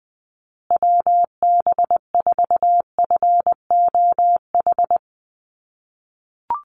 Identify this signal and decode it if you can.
Morse code, heard as a keyed tone, about 20 wpm, 700 Hz. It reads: WB4FOH
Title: USA Call Signs Repeated 3x 20wpm